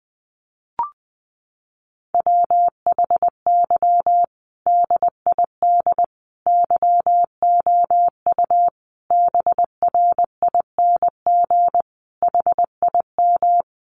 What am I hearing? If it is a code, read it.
Morse code, heard as a keyed tone, about 19 wpm, 700 Hz. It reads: WHY DID YOU BRING HIM
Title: Sentences from Top 500 Words 20wpm